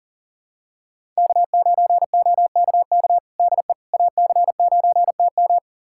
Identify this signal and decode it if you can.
Morse code, heard as a keyed tone, about 40 wpm, 700 Hz. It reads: K9OKK DE AC9TM
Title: USA Callsign DE Callsign - 5 Characters - 40wpm